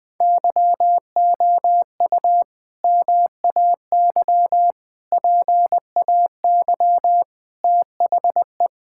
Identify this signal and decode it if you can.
Morse code, heard as a keyed tone, about 21 wpm, 700 Hz. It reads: YOU MAY PAY THE
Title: Sets of 8 Words - 20wpm